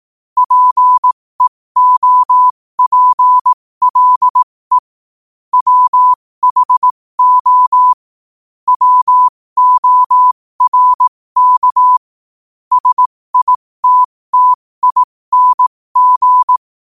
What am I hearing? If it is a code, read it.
Morse code, heard as a keyed tone, about 18 wpm, spaced as for 15 wpm, 1000 Hz. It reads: PEOPLE WHO WORK SITTING
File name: qotdfiles / 15wpm